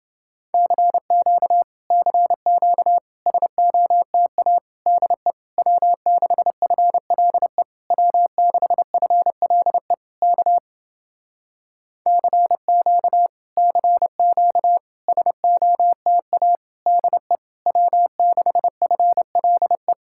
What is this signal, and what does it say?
Morse code, heard as a keyed tone, about 30 wpm, 700 Hz. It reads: CQ CQ SOTA DE W6FLE W6FLE K CQ CQ SOTA DE W6FLE